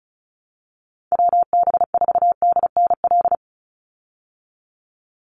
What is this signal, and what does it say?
Morse code, heard as a keyed tone, about 34 wpm, 700 Hz. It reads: WB4DNL
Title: Call Signs Repeated 2x - 35wpm